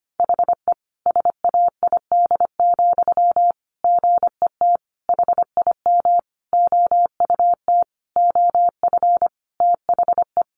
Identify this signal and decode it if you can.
Morse code, heard as a keyed tone, about 25 wpm, 700 Hz. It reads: HE SAID, GET HIM OUT OF THE
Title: Sentences from Top 400 Words 25wpm